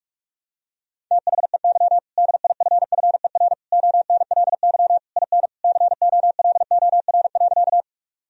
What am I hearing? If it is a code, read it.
Morse code, heard as a keyed tone, about 41 wpm, 700 Hz. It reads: THEY DIFFER ONLY IN COLOR.
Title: Sentences from Top 1000 Words - Review Part 2 - 45wpm